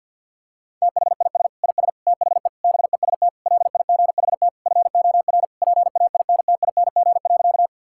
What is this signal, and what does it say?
Morse code, heard as a keyed tone, about 50 wpm, 700 Hz. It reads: THIS IS THE BEST LIGHT FOR PAINTING.
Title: Sentences from Top 1000 Words - Review Part 6 - 50wpm